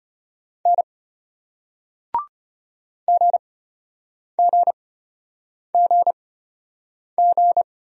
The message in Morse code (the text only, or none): N GGGG